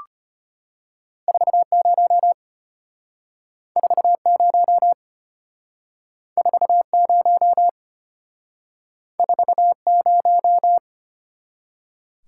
Morse code, heard as a keyed tone, about 33 wpm, 700 Hz. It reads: 40 40 40 40